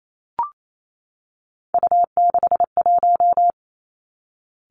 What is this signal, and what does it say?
Morse code, heard as a keyed tone, about 28 wpm, 700 Hz. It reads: U61
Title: Three Challenging Letter-Number Combinations 28wpm